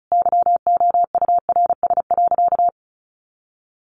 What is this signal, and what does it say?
Morse code, heard as a keyed tone, about 34 wpm, 700 Hz. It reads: YOURS.
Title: A Sound Of Thunder 35wpm